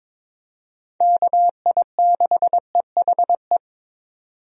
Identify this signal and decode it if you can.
Morse code, heard as a keyed tone, about 22 wpm, 700 Hz. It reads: KI6EHE
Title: USA Call Signs - 6 Characters - 22wpm